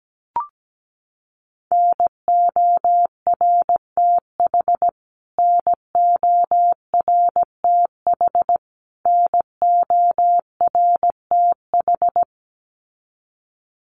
Morse code, heard as a keyed tone, about 17 wpm, 700 Hz. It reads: NORTH NORTH NORTH
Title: Top 500 Words Repeated 17wpm